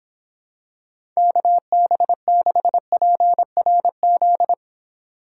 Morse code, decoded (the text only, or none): KB6PRZ